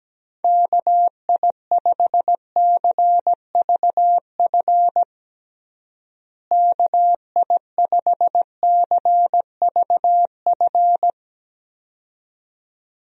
Morse code, heard as a keyed tone, about 17 wpm, 700 Hz. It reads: KI5CVF KI5CVF
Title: USA Call Signs - 6 Characters - Repeated 2x - 17wpm